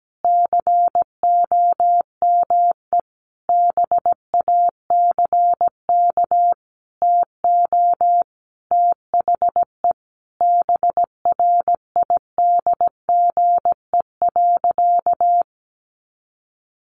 Morse code, heard as a keyed tone, about 17 wpm, 700 Hz. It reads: COME BACK TO THE BRIDGE.
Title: Sentences from Top 1200 Words 17wpm